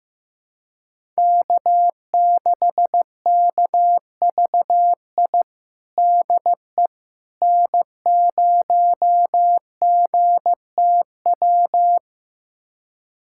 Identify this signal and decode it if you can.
Morse code, heard as a keyed tone, about 15 wpm, 700 Hz. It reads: K6KVI DE N0GTW
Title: USA Callsign DE Callsign - 5 Characters - 15wpm